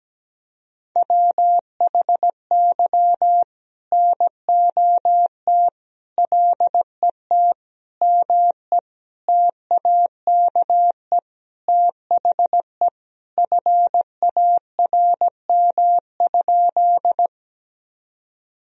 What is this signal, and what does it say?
Morse code, heard as a keyed tone, about 17 wpm, 700 Hz. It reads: WHY NOT LET ME TAKE THE FARM?
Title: Sentences from Top 1000 Words - Review Part 2 - 17wpm